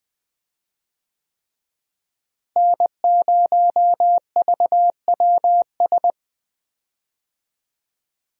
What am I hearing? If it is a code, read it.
Morse code, heard as a keyed tone, about 19 wpm, 700 Hz. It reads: N0VWS
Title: USA Call Signs - 5 Characters - Repeated 3x - 20wpm